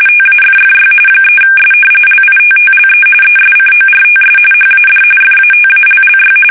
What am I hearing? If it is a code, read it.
RTTY, about 45.45 baud. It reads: TAFFY 3S LITTLE ESCORTS HAD FATALLY
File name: fsk850 / battleshipnj